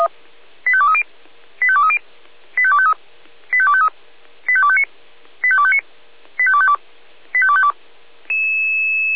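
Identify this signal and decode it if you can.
Five-tone selective call (ZVEI), heard as quick runs of five tones, tones of about 70 ms. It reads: 85230 85230 85253 85253 85259 85259 85252 85252
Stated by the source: Received on SDR as a ZVEI five-tone selective call, code 85267.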